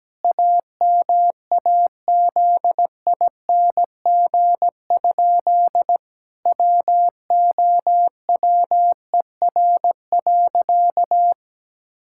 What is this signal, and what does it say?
Morse code, heard as a keyed tone, about 17 wpm, 700 Hz. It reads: AMAZING? WOWER.
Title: Pangram Challenge v2 - 17wpm